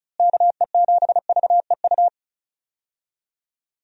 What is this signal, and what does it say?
Morse code, heard as a keyed tone, about 36 wpm, 700 Hz. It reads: KE7VEU
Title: USA Call Signs - 6 Characters - Repeated 2x - 35wpm